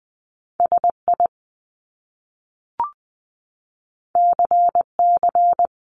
Morse code, heard as a keyed tone, about 20 wpm, 700 Hz. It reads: SI CC
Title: Top Two Letter Combinations in Words 20wpm